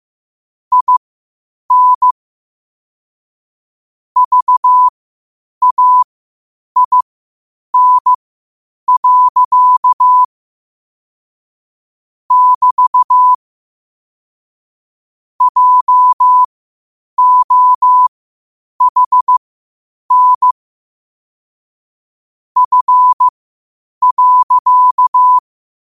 Morse code, heard as a keyed tone, about 15 wpm, spaced as for 8 wpm, 1000 Hz. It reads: IN VAIN. <BT> JOHN F.
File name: qotdfiles / 8wpm